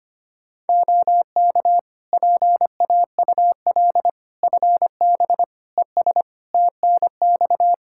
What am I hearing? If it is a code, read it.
Morse code, heard as a keyed tone, about 25 wpm, 700 Hz. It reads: OK PAUL FB ES TNX
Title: Practice QSO - Exchange 2 - Info - 25wpm